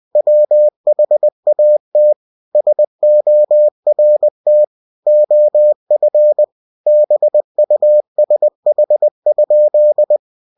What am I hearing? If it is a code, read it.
Morse code, heard as a keyed tone, about 20 wpm, 600 Hz. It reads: WHAT SORT OF BUSH?
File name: cw-books / winnie-the-pooh / winnie-20wpm-page-0082